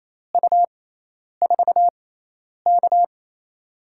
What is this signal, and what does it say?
Morse code, heard as a keyed tone, about 28 wpm, spaced as for 10 wpm, 700 Hz. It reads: U4K
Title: USA Call Signs - 4 Characters - 6x Character Spacing - 28wpm